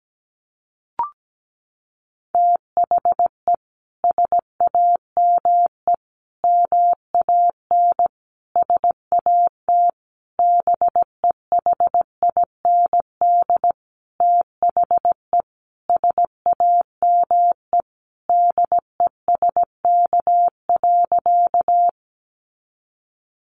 Morse code, heard as a keyed tone, about 17 wpm, 700 Hz. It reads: THE SAME MAN SAT BEHIND THE SAME DESK.
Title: A Sound Of Thunder 17wpm